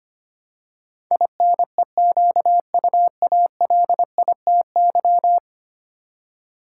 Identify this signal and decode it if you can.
Morse code, heard as a keyed tone, about 25 wpm, 700 Hz. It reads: INEQUALITY
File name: 10 Letter Words - First 3 Letters Given - 25wpm